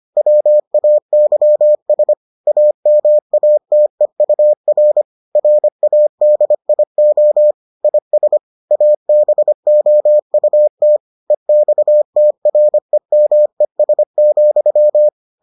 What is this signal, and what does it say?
Morse code, heard as a keyed tone, about 25 wpm, 600 Hz. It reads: WAYS AMATEUR RADIO IS ABOUT EXTREMES,